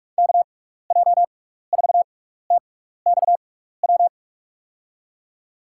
Morse code, heard as a keyed tone, about 43 wpm, spaced as for 15 wpm, 700 Hz. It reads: KJ4TXW
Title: USA Call Signs - 6x Character Spacing - 45wpm